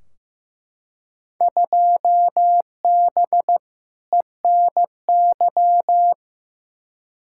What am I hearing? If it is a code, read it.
Morse code, heard as a keyed tone, about 15 wpm, 700 Hz. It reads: 2B ENY
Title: Field Day Exchanges Encore - 15wpm